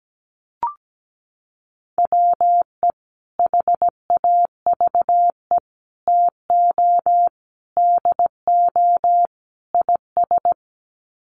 Morse code, heard as a keyed tone, about 17 wpm, 700 Hz. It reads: WE HAVE TO DO IS
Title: Sets of 5 Words Encore 17wpm